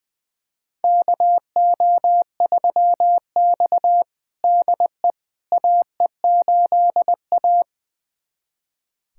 Morse code, heard as a keyed tone, about 20 wpm, 700 Hz. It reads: KO3X DE AE8A
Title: USA Callsign DE Callsign - 4 Characters - 20wpm